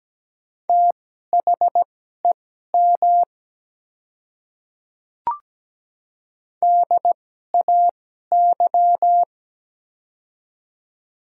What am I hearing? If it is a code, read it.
Morse code, heard as a keyed tone, about 17 wpm, spaced as for 12 wpm, 700 Hz. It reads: THEM DAY
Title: 2x Character Spacing - Top 100 Words - 17wpm